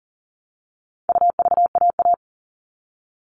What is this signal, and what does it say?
Morse code, heard as a keyed tone, about 40 wpm, 700 Hz. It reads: UVAA
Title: All Cut Numbers - 4-Digit 40wpm